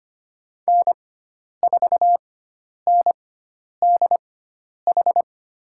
Morse code, heard as a keyed tone, about 25 wpm, spaced as for 10 wpm, 700 Hz. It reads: N4NDH
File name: Lesson 16g - Letter c - Call Signs 25wpm